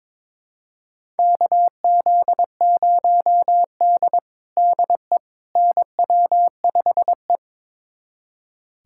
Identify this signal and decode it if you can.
Morse code, heard as a keyed tone, about 22 wpm, 700 Hz. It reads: KZ0D DE NW5E
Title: USA Callsign DE Callsign - 4 Characters - 22wpm